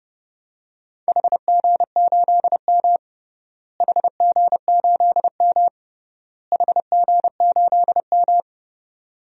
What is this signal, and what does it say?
Morse code, heard as a keyed tone, about 30 wpm, 700 Hz. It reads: HG8M HG8M HG8M
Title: DXCC-100 Call Signs - 4 Characters - Repeated 3x - 30wpm